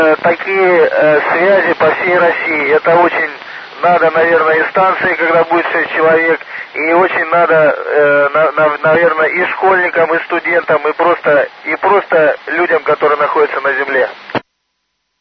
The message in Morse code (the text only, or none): none